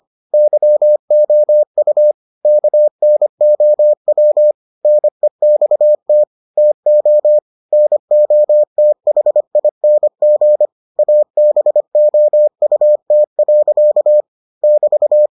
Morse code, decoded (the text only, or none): YOU KNOW NEXT TO NOTHING ABOUT. <BT>